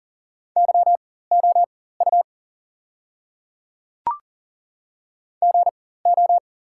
Morse code, heard as a keyed tone, about 40 wpm, spaced as for 19 wpm, 700 Hz. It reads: YOU GO
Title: 4x Character Spacing - Top 100 Words - 40wpm